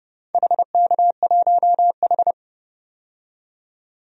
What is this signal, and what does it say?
Morse code, heard as a keyed tone, about 30 wpm, 700 Hz. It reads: HK1H
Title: DXCC-100 Call Signs - 4 Characters - 30wpm